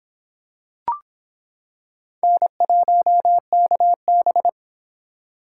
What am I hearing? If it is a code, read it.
Morse code, heard as a keyed tone, about 25 wpm, 700 Hz. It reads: N1KB